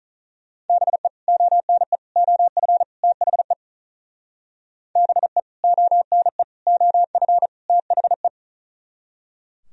Morse code, heard as a keyed tone, about 45 wpm, 700 Hz. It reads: BE ONE OF THE BE ONE OF THE